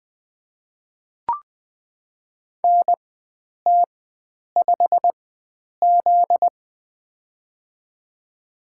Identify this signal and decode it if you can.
Morse code, heard as a keyed tone, about 20 wpm, spaced as for 9 wpm, 700 Hz. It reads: NT5Z
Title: USA Call Signs - 4x Character Spacing - 20wpm